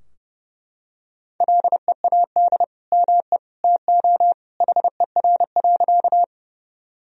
Morse code, heard as a keyed tone, about 29 wpm, 700 Hz. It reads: LEAD ME TO HER.
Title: Sentences from Top 1000 Words - Review Part 2 - 30wpm